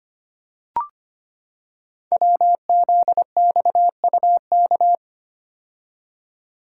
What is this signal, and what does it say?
Morse code, heard as a keyed tone, about 25 wpm, 700 Hz. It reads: WZXUK
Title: Code Groups - 5 Letter:Numbers - 25wpm